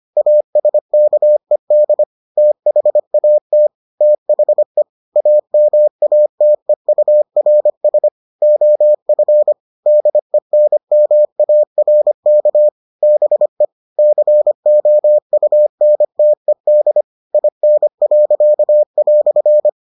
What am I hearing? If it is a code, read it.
Morse code, heard as a keyed tone, about 25 wpm, 600 Hz. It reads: ASKED THAT THE AMATEURS OF DENMARK BE COUNTED IN."